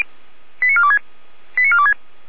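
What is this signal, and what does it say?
Five-tone selective call (ZVEI), heard as quick runs of five tones, tones of about 70 ms. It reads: 89426 89426
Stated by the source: ZVEI five-tone selective call received by SDR, archived by Feuerwehr Orlamünde.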